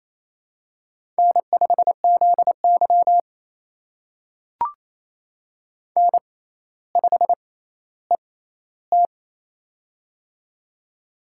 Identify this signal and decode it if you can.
Morse code, heard as a keyed tone, about 28 wpm, spaced as for 14 wpm, 700 Hz. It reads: N5ZY N5ET
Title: USA Call Signs - 4 Characters - 6x Character Spacing - 28wpm